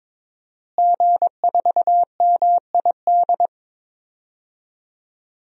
Morse code, heard as a keyed tone, about 22 wpm, 700 Hz. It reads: G4MID
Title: DXCC-100 Call Signs - 5 Characters - Repeated 3x - 22wpm